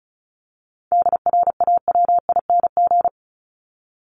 Morse code, heard as a keyed tone, about 34 wpm, 700 Hz. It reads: DRAWING
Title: Top 1751-2000 Words 35wpm